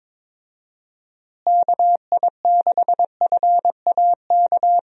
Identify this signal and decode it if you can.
Morse code, heard as a keyed tone, about 22 wpm, 700 Hz. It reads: KI6FAK
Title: USA Call Signs - 6 Characters - Repeated 3x - 22wpm